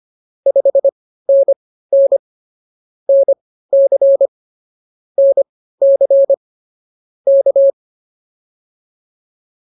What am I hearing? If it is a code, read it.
Morse code, heard as a keyed tone, about 24 wpm, spaced as for 15 wpm, 550 Hz. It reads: 5NN NC NC K